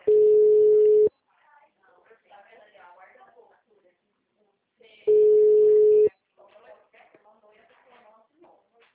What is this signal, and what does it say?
Ringback tone (Europe, 425 Hz), with speech.
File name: out- -2025- - - .59404